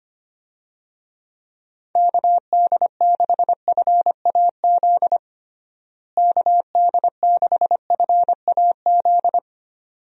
Morse code, heard as a keyed tone, about 25 wpm, 700 Hz. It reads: KD6FAZ KD6FAZ
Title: USA Call Signs - 6 Characters - Repeated 2x - 25wpm